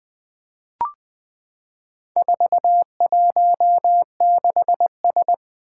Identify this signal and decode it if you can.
Morse code, heard as a keyed tone, about 20 wpm, 700 Hz. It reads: 416S
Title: Four Challenging Letter-Number Combinations 20wpm